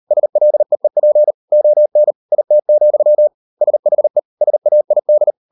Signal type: Morse code